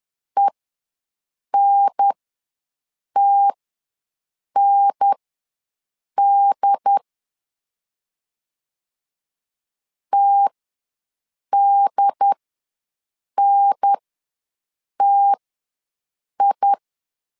Morse code, heard as a keyed tone, about 10 wpm, spaced as for 6 wpm, 800 Hz. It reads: ENTND TDNTI
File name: LEZIONI / 2 UHDN / 2 UHDN ws 10wpm 2044